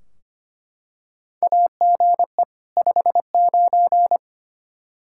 Morse code, heard as a keyed tone, about 25 wpm, 700 Hz. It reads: AGE 59
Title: K1USN SST QSO Practice 25wpm